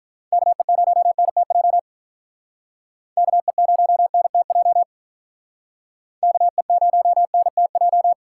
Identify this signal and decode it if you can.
Morse code, heard as a keyed tone, about 45 wpm, 700 Hz. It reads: KE0NTJ KE0NTJ KE0NTJ